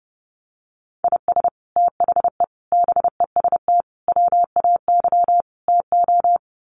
Morse code, heard as a keyed tone, about 30 wpm, 700 Hz. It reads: IS THE BEST WAY TO
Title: Sets of 5 Words Encore 30wpm